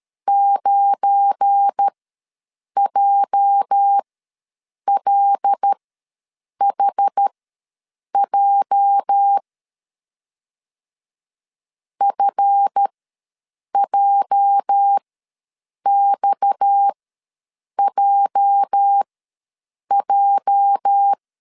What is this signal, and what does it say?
Morse code, heard as a keyed tone, about 13 wpm, spaced as for 7 wpm, 800 Hz. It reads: 9JLHJ FJXJJ
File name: LEZIONI / 8 YJ45 / 8 YJ ws 12wpm 3342